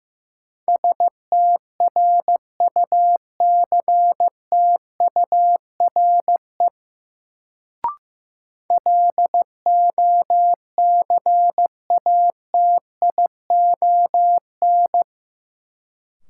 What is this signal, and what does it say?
Morse code, heard as a keyed tone, about 15 wpm, 700 Hz. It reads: STRUCTURE LOCATION
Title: Sentences from Top 1000 Words - Review Part 6 - New Word Drill - 15wpm